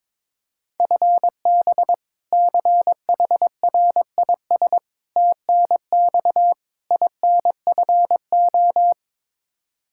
Morse code, heard as a keyed tone, about 22 wpm, 700 Hz. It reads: FB CHRIS TNX INFO
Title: Practice QSO - Exchange 3 - More Info - 22wpm